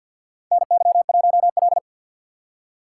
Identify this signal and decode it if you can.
Morse code, heard as a keyed tone, about 53 wpm, 700 Hz. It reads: NY1L
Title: USA Call Signs - 4 Characters - 6x Character Spacing - 50wpm